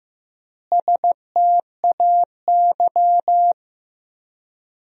Morse code, heard as a keyed tone, about 15 wpm, 700 Hz. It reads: STAY
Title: 4x Word-Spacing - Sentences from Top 500 Words - 15wpm